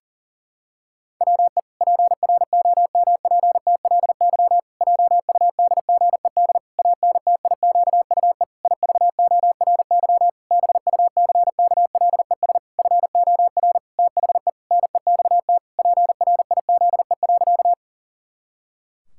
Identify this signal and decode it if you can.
Morse code, heard as a keyed tone, about 40 wpm, 700 Hz. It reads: WE PROMPTLY JUDGED ANTIQUE IVORY BUCKLES FOR THE NEXT PRIZE.
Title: Pangram Challenge v2 - 40wpm